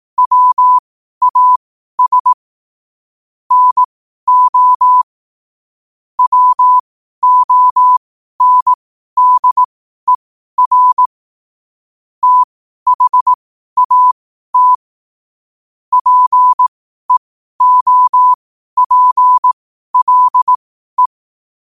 Morse code, heard as a keyed tone, about 18 wpm, spaced as for 12 wpm, 1000 Hz. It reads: WAS NO WONDER THAT PEOPLE
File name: qotdfiles / 12wpm